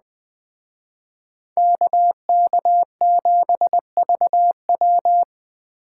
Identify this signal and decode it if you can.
Morse code, heard as a keyed tone, about 20 wpm, 700 Hz. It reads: KK7VW